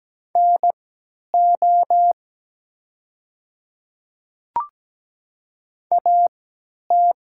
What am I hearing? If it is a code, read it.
Morse code, heard as a keyed tone, about 17 wpm, spaced as for 10 wpm, 700 Hz. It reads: NO AT
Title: 3x Character Spacing - Top 100 Words - 17wpm